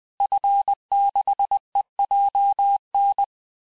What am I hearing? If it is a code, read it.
Morse code, heard as a keyed tone, about 20 wpm, 800 Hz. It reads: F6EJN
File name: cours-du-mercredi / audios / cours-858-Fin-de-QSO-vitesse-19mpm